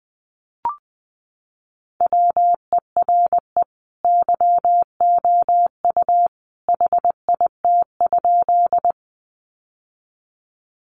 Morse code, heard as a keyed tone, about 20 wpm, 700 Hz. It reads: WERE YOU HIT?
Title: Sentences from Top 1200 Words 20wpm